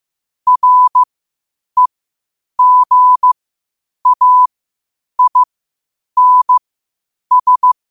Morse code, heard as a keyed tone, about 15 wpm, spaced as for 8 wpm, 1000 Hz. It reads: REGAINS
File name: qotdfiles / 8wpm